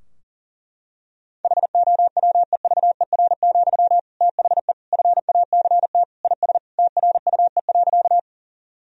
Morse code, heard as a keyed tone, about 37 wpm, 700 Hz. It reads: HOWEVER, THE FACT IS TRUE.
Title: Sentences from Top 1000 Words - Review Part 3 - 40wpm